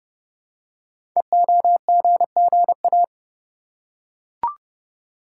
Morse code, heard as a keyed tone, about 29 wpm, 700 Hz. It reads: EOGGA
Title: Code Groups - 5 Letters - 30wpm